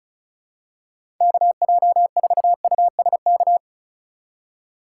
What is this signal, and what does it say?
Morse code, heard as a keyed tone, about 36 wpm, 700 Hz. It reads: KJ4USK
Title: USA Call Signs - 6 Characters - 2x Character Spacing - 35wpm